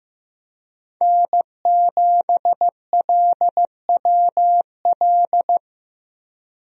Morse code, heard as a keyed tone, about 15 wpm, 700 Hz. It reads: N7LWL
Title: USA Call Signs - 5 Characters - Repeated 3x - 15wpm